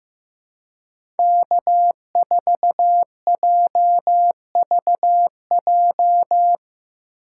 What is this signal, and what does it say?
Morse code, heard as a keyed tone, about 15 wpm, 700 Hz. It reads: K4JVJ
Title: USA Call Signs - 6x Character Spacing - 15wpm